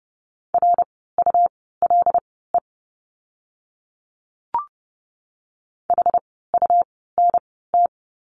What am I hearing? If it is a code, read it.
Morse code, heard as a keyed tone, about 30 wpm, spaced as for 17 wpm, 700 Hz. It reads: RULE HUNT
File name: Lesson 17f - Letter u - Words 30wpm